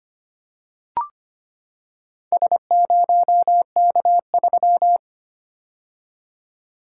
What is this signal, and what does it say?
Morse code, heard as a keyed tone, about 25 wpm, 700 Hz. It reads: S0K3
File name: practice / 25wpm / Four Letter-Number Combinations - 25wpm